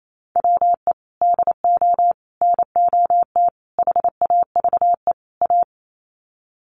Morse code, heard as a keyed tone, about 28 wpm, 700 Hz. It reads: WE DO NOT HAVE A
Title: Sets of 5 Words 28wpm